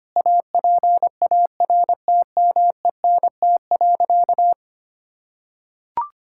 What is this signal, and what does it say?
Morse code, heard as a keyed tone, about 25 wpm, 700 Hz. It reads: APARTMENT.
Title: Sentences from Top 1000 Words - Review Part 6 - 25wpm